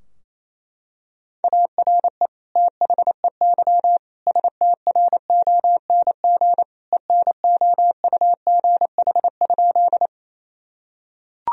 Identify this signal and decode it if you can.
Morse code, heard as a keyed tone, about 28 wpm, 700 Hz. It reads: ARE THEY STRONG ENOUGH?
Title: Sentences from Top 1000 Words - Review Part 3 - 28wpm